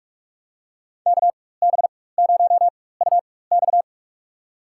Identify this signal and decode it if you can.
Morse code, heard as a keyed tone, about 43 wpm, spaced as for 21 wpm, 700 Hz. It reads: KB0UX
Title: USA Call Signs - 5 Characters - 4x Character Spacing - 45wpm